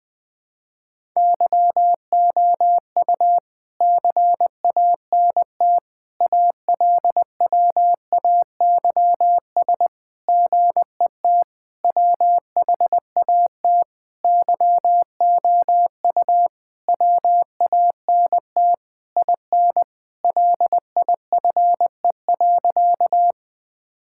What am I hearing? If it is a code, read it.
Morse code, heard as a keyed tone, about 20 wpm, 700 Hz. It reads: YOU CANT ALWAYS GET WHAT YOU WANT IN LIFE.
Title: Sentences with 7-Word Phrases - 20wpm